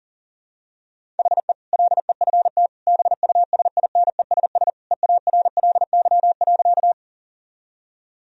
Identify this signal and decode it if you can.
Morse code, heard as a keyed tone, about 40 wpm, 700 Hz. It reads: HE LEFT BUSINESS EARLY.
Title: Sentences from Top 1000 Words - Review Part 2 - 40wpm